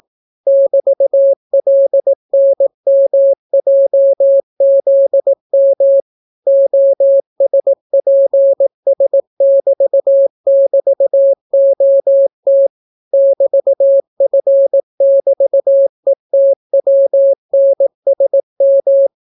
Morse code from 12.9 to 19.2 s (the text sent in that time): <BT>F<BT>ETWNSM